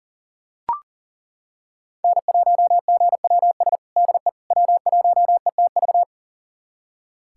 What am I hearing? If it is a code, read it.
Morse code, heard as a keyed tone, about 37 wpm, 700 Hz. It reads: N1GWS DE W1ETV
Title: USA Callsign DE Callsign - 5 Characters - 40wpm